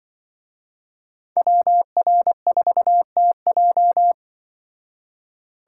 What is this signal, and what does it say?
Morse code, heard as a keyed tone, about 24 wpm, 700 Hz. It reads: WR4TJ
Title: Speed Racing - USA 2x2 Call Signs 20wpm